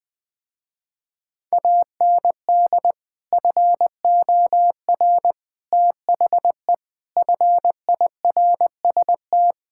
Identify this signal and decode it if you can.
Morse code, heard as a keyed tone, about 20 wpm, 700 Hz. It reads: AND FOR THE FIRST
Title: Sentences with 5-Word Phrases - Spoken Beginning - 20wpm